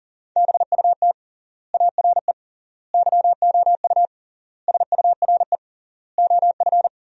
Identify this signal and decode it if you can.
Morse code, heard as a keyed tone, about 40 wpm, 700 Hz. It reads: BUT ARE YOU SURE OF
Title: 3x Word-Spacing - Sentences from Top 300 Words - 40wpm